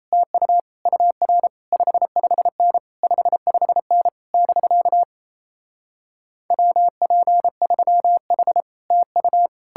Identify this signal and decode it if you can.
Morse code, heard as a keyed tone, about 38 wpm, 700 Hz. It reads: TU UR 55N 55N <BK> WP3H TU